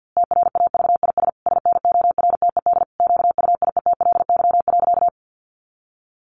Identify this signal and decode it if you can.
Morse code, heard as a keyed tone, about 50 wpm, 700 Hz. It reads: TRAVIS SNORTED QUIETLY.
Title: A Sound Of Thunder 50wpm